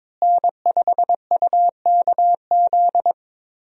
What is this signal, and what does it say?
Morse code, heard as a keyed tone, about 22 wpm, 700 Hz. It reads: N5UKZ
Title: USA Call Signs - 5 Characters - Repeated 2x - 22wpm